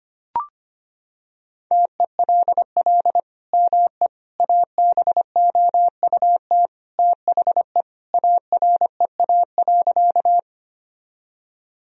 Morse code, heard as a keyed tone, about 25 wpm, 700 Hz. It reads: TELL ME ABOUT THE AREA.
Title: Sentences from Top 1000 Words - Review Part 2 - 25wpm